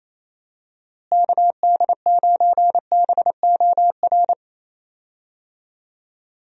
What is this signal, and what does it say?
Morse code, heard as a keyed tone, about 28 wpm, 700 Hz. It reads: KD9BOR
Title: USA Call Signs - 6 Characters - Repeated 3x - 28wpm